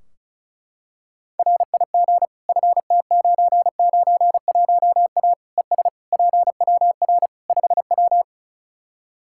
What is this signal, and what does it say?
Morse code, heard as a keyed tone, about 36 wpm, 700 Hz. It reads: RIG FT991A ES PWR 5W
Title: Practice QSO - Exchange 2 - Info - 35wpm